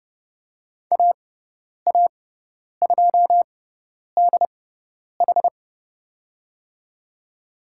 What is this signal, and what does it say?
Morse code, heard as a keyed tone, about 30 wpm, spaced as for 10 wpm, 700 Hz. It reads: AA2DH
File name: Lesson 18g - Letter m - Call Signs 30wpm